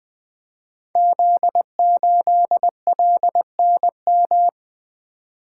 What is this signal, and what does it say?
Morse code, heard as a keyed tone, about 20 wpm, 700 Hz. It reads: Z8LNM
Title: Code Groups - 5 Letter:Numbers - 20wpm